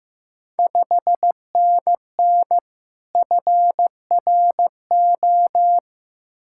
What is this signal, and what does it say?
Morse code, heard as a keyed tone, about 15 wpm, 690 Hz. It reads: 5NN FRO
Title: Ontario QSO Party Exchange 15wpm